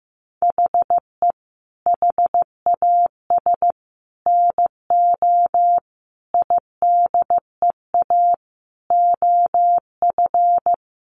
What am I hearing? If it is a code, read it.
Morse code, heard as a keyed tone, about 15 wpm, 700 Hz. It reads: HE HAS NO IDEA OF
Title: Sentences from Top 1400 Words 15wpm